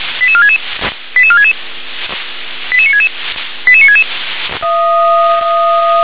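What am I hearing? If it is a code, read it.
Five-tone selective call (ZVEI), heard as quick runs of five tones, tones of about 70 ms. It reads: 88466 88466 88966 88966
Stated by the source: ZVEI five-tone selective call received with an SDR.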